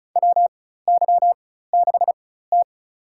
Morse code, heard as a keyed tone, about 36 wpm, spaced as for 16 wpm, 700 Hz. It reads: WY6T
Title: USA Call Signs - 4 Characters - 4x Character Spacing - 35wpm